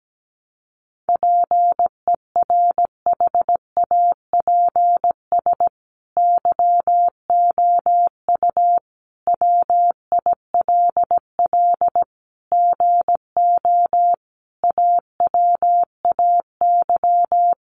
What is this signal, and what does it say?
Morse code, heard as a keyed tone, about 17 wpm, 700 Hz. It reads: PERHAPS YOU WILL GO AWAY
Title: Sentences from Top 1300 Words 17wpm